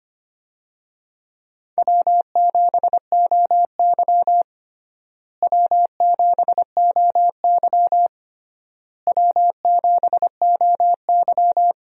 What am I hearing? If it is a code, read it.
Morse code, heard as a keyed tone, about 25 wpm, 700 Hz. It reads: W7OY W7OY W7OY
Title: USA Call Signs - 4 Characters - Repeated 3x - 25wpm